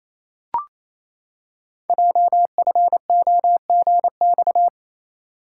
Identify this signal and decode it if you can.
Morse code, heard as a keyed tone, about 27 wpm, 700 Hz. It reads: JFOGX
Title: Code Groups - 5 Letters - 28wpm